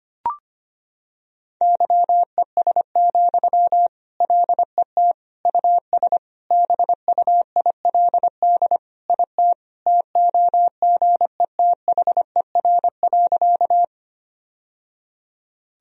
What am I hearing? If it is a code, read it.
Morse code, heard as a keyed tone, about 25 wpm, 700 Hz. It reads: YES, LET US BUILD IT TOGETHER.
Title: Sentences from Top 1000 Words - Review Part 1 - 25wpm